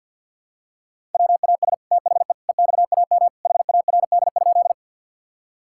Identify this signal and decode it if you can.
Morse code, heard as a keyed tone, about 46 wpm, 700 Hz. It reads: WAS THE EXAM HARD?
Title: Sentences from Top 1000 Words - Review Part 5 - 50wpm